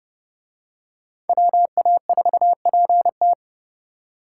Morse code, heard as a keyed tone, about 27 wpm, 700 Hz. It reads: WA4PT